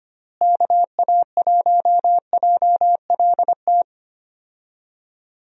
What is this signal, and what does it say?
Morse code, heard as a keyed tone, about 25 wpm, 700 Hz. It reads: KA1JLT
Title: USA Call Signs - 6 Characters - Repeated 2x - 25wpm